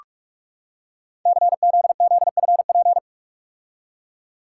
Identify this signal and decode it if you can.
Morse code, heard as a keyed tone, about 44 wpm, 700 Hz. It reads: CZZFP
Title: Code Groups - 5 Letter:Numbers - 45wpm